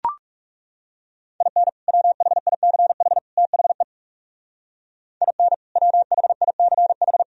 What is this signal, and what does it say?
Morse code, heard as a keyed tone, about 38 wpm, 700 Hz. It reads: IN WHICH THE IN WHICH
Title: Speed-Racing - Sets of 3 Words 30wpm